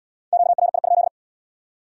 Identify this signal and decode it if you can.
Morse code, heard as a keyed tone, about 72 wpm, 700 Hz. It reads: QRI?